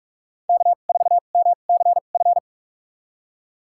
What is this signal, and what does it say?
Morse code, heard as a keyed tone, about 47 wpm, spaced as for 33 wpm, 700 Hz. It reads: K4MCF
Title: USA Call Signs - 5 Characters - 2x Character Spacing - 45wpm